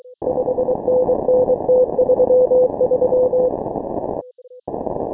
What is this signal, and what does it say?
Morse code, heard as a keyed tone, about 23 wpm, 530 Hz. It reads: 5NN T33
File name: final / W1UJ-168-140-MorseRunner